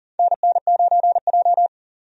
Morse code, heard as a keyed tone, about 40 wpm, 700 Hz. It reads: NN9J
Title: USA Call Signs - 4 Characters - 40wpm